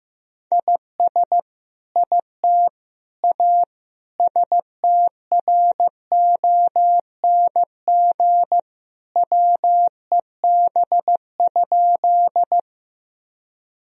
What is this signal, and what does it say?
Morse code, heard as a keyed tone, about 15 wpm, 700 Hz. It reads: IS IT A STRONG WEB?
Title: Sentences from Top 1000 Words - Review Part 4 - 15wpm